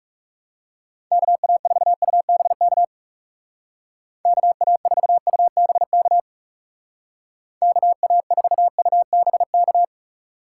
Morse code, heard as a keyed tone, about 36 wpm, 700 Hz. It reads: KA4UBK KA4UBK KA4UBK
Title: Speed Racing - USA 2x3 Call Signs 30wpm